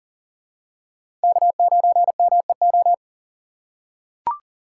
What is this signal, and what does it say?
Morse code, heard as a keyed tone, about 40 wpm, 700 Hz. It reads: K9MEO
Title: USA Call Signs - 4x Character Spacing - 40wpm